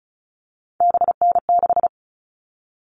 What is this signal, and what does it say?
Morse code, heard as a keyed tone, about 34 wpm, 700 Hz. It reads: BN6
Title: All Cut Numbers - 3-Digit 35wpm